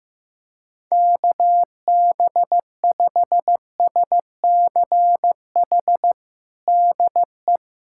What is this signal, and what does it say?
Morse code, heard as a keyed tone, about 15 wpm, 700 Hz. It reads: KB5SCH DE
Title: USA Callsign DE Callsign - 6 Characters - 15wpm